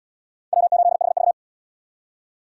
Morse code, heard as a keyed tone, about 67 wpm, 700 Hz. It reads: W8HV